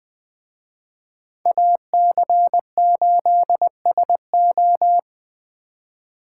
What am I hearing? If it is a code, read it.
Morse code, heard as a keyed tone, about 21 wpm, 700 Hz. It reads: AC8SO